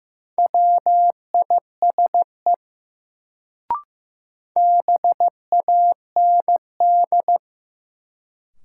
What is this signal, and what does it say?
Morse code, heard as a keyed tone, about 15 wpm, 700 Hz. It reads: WISE BAND
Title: Sentences from Top 1600 Words - New Word Drill Alt - 15wpm